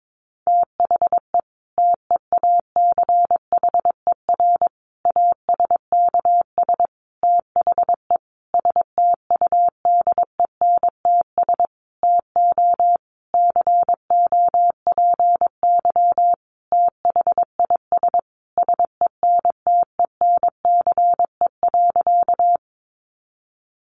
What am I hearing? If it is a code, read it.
Morse code, heard as a keyed tone, about 22 wpm, 700 Hz. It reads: THE TEACHER ASKS THE STUDENTS TO COPY THIS SENTENCE.
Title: Additional Easy Phrases 22wpm